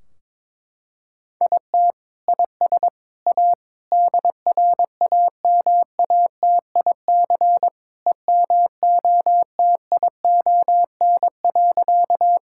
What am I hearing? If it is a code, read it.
Morse code, heard as a keyed tone, about 21 wpm, 700 Hz. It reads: IT IS A DRAMATIC EMOTION.
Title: Sentences from Top 1600 Words 22wpm